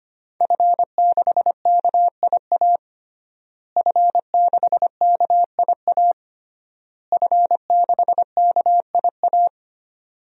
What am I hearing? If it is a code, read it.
Morse code, heard as a keyed tone, about 25 wpm, 700 Hz. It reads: F6KIA F6KIA F6KIA
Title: DXCC-100 Call Signs - 5 Characters - Repeated 3x - 25wpm